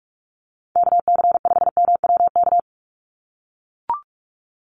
Morse code, heard as a keyed tone, about 44 wpm, 700 Hz. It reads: KC5MWK
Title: USA Call Signs 45wpm